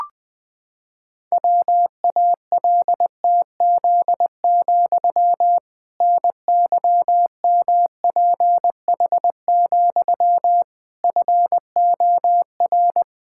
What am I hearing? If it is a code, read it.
Morse code, heard as a keyed tone, about 21 wpm, 700 Hz. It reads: WALTZ, NYMPH, FOR
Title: Pangram Challenge v2 - 20wpm